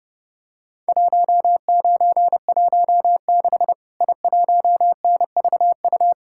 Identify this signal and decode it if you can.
Morse code, heard as a keyed tone, about 30 wpm, 700 Hz. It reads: 1916 I1NVU
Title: Worked All Europe DX Contest - QTC Exchange - 30wpm